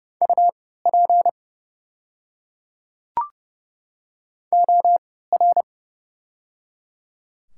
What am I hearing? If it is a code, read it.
Morse code, heard as a keyed tone, about 30 wpm, spaced as for 17 wpm, 700 Hz. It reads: UP OR
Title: 3x Character Spacing - Top 200 Words - 30wpm